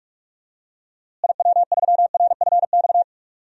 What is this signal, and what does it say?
Morse code, heard as a keyed tone, about 45 wpm, 700 Hz. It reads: IW3RFX
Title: DXCC-100 Call Signs - 6 Characters - Repeated 2x - 45wpm